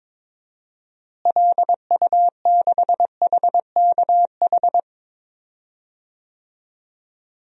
Morse code, heard as a keyed tone, about 22 wpm, 700 Hz. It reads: LU6HKH
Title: DXCC-100 Call Signs - 6 Characters - Repeated 3x - 22wpm